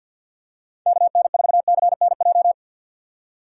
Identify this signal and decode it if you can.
Morse code, heard as a keyed tone, about 53 wpm, 700 Hz. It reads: KN4CNJ
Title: USA Call Signs - 6 Characters - 4x Character Spacing - 50wpm